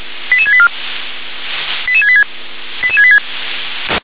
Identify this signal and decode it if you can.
Five-tone selective call (ZVEI), heard as quick runs of five tones, tones of about 70 ms. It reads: 88684 88686
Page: ZVEI